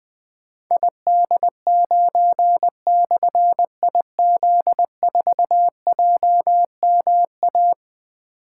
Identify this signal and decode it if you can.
Morse code, heard as a keyed tone, about 21 wpm, 700 Hz. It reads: ID9/IZ4JMA
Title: Difficult Call Signs 20wpm